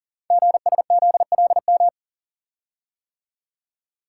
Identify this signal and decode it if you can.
Morse code, heard as a keyed tone, about 41 wpm, 700 Hz. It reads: GSZLM
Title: Code Groups - 5 Letters - 40wpm